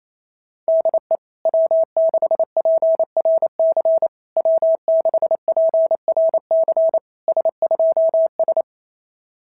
Morse code, heard as a keyed tone, about 28 wpm, 660 Hz. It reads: DE W6PRC W6PRC S2S
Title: SOTA QSO Practice 28wpm